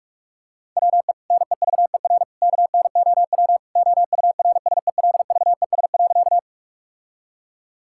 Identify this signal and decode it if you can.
Morse code, heard as a keyed tone, about 46 wpm, 700 Hz. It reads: WE NEVER KNOW OURSELVES.
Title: Sentences from Top 1600 Words 45wpm